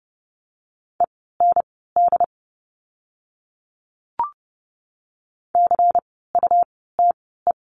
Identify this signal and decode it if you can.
Morse code, heard as a keyed tone, about 30 wpm, spaced as for 17 wpm, 700 Hz. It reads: END CUTE
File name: Lesson 17f - Letter u - Words 30wpm